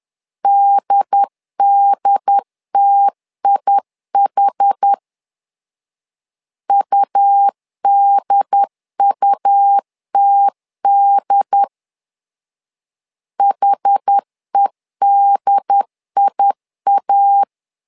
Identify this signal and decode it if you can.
Morse code, heard as a keyed tone, about 10 wpm, 800 Hz. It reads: DDTIH UDUTD HEDIA
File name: LEZIONI / 2 UHDN / 2 UHD 10wpm 1826